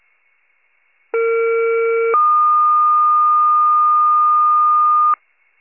Two-tone page, heard as a one-second tone, then a three-second tone, about 460 then 1220 Hz. It is heard from 1.1 to 5.1 s.
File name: 21 RESCUE